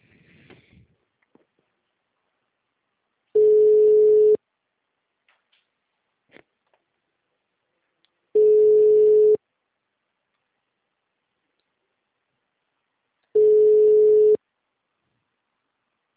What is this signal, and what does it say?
Ringback tone (Europe, 425 Hz).
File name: out- -2002- - - .117